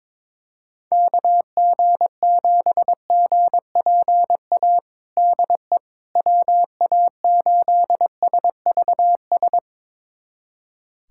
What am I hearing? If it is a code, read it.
Morse code, heard as a keyed tone, about 22 wpm, 700 Hz. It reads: KG7GPA DE WA8SVS
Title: USA Callsign DE Callsign 22wpm